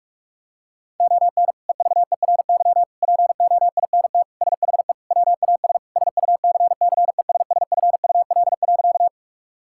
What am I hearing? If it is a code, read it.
Morse code, heard as a keyed tone, about 46 wpm, 700 Hz. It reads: ON EVERY POINT SHE WAS SUCCESSFUL.
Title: Sentences from Top 1000 Words - Review Part 6 - 45wpm